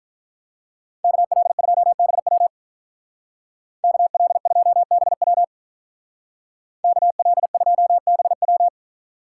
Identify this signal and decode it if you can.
Morse code, heard as a keyed tone, about 43 wpm, 700 Hz. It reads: KL2BW KL2BW KL2BW
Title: Speed Racing - USA 2x2 Call Signs 35wpm